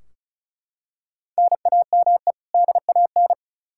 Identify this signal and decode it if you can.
Morse code, heard as a keyed tone, about 36 wpm, 700 Hz. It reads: NAME DAN
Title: Practice QSO - Exchange 1 - Beginning - 35wpm